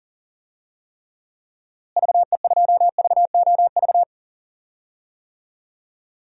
Morse code, heard as a keyed tone, about 37 wpm, 700 Hz. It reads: VE2VOV
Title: DXCC-100 Call Signs - 6 Characters - Repeated 3x - 40wpm